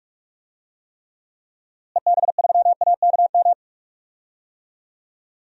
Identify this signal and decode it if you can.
Morse code, heard as a keyed tone, about 41 wpm, 700 Hz. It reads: EB3AKM
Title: DXCC-100 Call Signs - 6 Characters - Repeated 3x - 45wpm